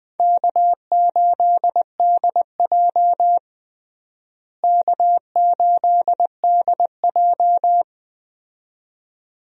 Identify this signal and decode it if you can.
Morse code, heard as a keyed tone, about 20 wpm, 700 Hz. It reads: K8DJ K8DJ
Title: USA Call Signs - 4 Characters - Repeated 2x - 20wpm